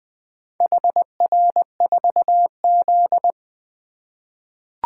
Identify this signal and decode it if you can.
Morse code, heard as a keyed tone, about 20 wpm, 700 Hz. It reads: HR4Z
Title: Four Letter-Number Combinations - 20wpm